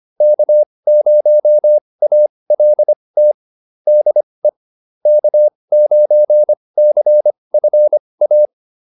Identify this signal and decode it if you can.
Morse code, heard as a keyed tone, about 25 wpm, spaced as for 20 wpm, 600 Hz. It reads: K0ALT DE K9CFA